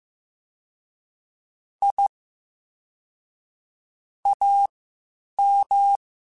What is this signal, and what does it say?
Morse code, heard as a keyed tone, about 15 wpm, spaced as for 8 wpm, 800 Hz. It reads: I AM
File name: QSOs / QSOs-mp3s / WD0FBT 1@13